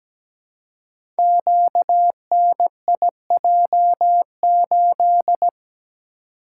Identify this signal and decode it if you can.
Morse code, heard as a keyed tone, about 17 wpm, 700 Hz. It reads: QNIJ8
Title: Code Groups - 5 Letter:Numbers - 17wpm